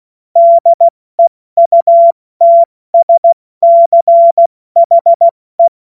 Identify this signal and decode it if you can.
Morse code, heard as a keyed tone, about 16 wpm, spaced as for 14 wpm, 680 Hz. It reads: DEUTSCHE
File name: Morsekurs / Witze / Witz 30. Dezember 2025 70 F80